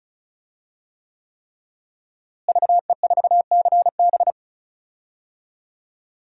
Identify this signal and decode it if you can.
Morse code, heard as a keyed tone, about 34 wpm, 700 Hz. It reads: VE4CB
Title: DXCC-100 Call Signs - 5 Characters - Repeated 3x - 35wpm